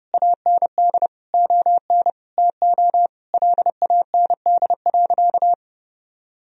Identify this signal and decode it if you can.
Morse code, heard as a keyed tone, about 30 wpm, 700 Hz. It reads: AND ON TO LAND.
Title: Sentences from Top 1000 Words - Review Part 2 - 30wpm